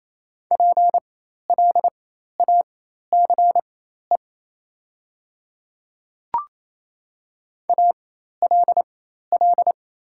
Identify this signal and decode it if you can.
Morse code, heard as a keyed tone, about 28 wpm, spaced as for 13 wpm, 700 Hz. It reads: PLACE ALL
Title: 4x Character Spacing - Top 200 Words - 28wpm